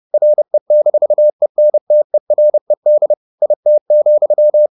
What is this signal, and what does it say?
Morse code, heard as a keyed tone, about 30 wpm, 600 Hz. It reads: RE-ENTERED IT,
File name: the old man and the sea / the old man and the sea 1 30wpm